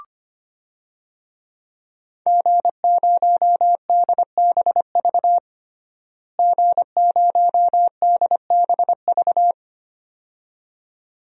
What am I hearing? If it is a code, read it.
Morse code, heard as a keyed tone, about 25 wpm, 700 Hz. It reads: G0DBV G0DBV
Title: DXCC-100 Call Signs - 5 Characters - Repeated 2x - 25wpm